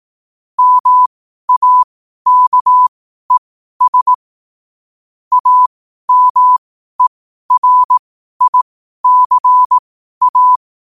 Morse code, heard as a keyed tone, about 18 wpm, spaced as for 12 wpm, 1000 Hz. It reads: MAKES AMERICA
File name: qotdfiles / 12wpm